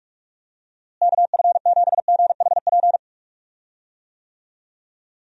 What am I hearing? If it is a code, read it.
Morse code, heard as a keyed tone, about 46 wpm, 700 Hz. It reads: KF7GHP
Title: USA Call Signs - 6 Characters - Repeated 2x - 45wpm